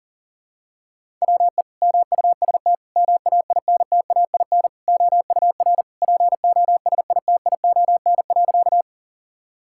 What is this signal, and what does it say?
Morse code, heard as a keyed tone, about 40 wpm, 700 Hz. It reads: WE MUST MAINTAIN OUR POSITION.
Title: Sentences from Top 1000 Words - Review Part 5 - 40wpm